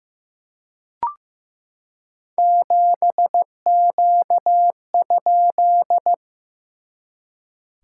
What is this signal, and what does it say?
Morse code, heard as a keyed tone, about 15 wpm, 700 Hz. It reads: 7Q?
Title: SOTA QSO Practice 15wpm